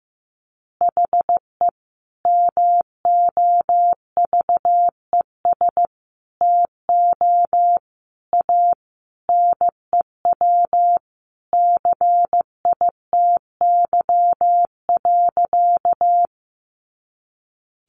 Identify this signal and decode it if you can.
Morse code, heard as a keyed tone, about 15 wpm, 700 Hz. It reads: HE MOVES TO A NEW CITY.
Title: Easy Phrases 15wpm